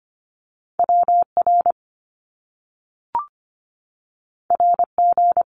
Morse code, heard as a keyed tone, about 25 wpm, 700 Hz. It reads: WR RG
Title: Top Two Letter Combinations in Words 25wpm